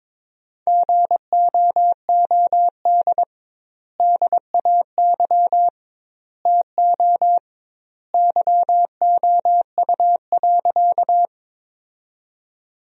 Morse code, heard as a keyed tone, about 22 wpm, 700 Hz. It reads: GOOD DAY TO YOU.
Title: 2x Word-Spacing - Sentences from Top 200 Words - 22wpm